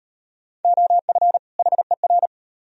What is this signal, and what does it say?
Morse code, heard as a keyed tone, about 36 wpm, 700 Hz. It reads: OF HER